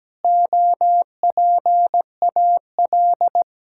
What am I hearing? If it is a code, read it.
Morse code, heard as a keyed tone, about 17 wpm, 700 Hz. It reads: OPAL
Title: Pangram Challenge v2 - 17wpm